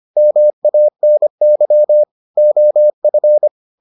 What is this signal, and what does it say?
Morse code, heard as a keyed tone, about 25 wpm, 600 Hz. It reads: MANY OF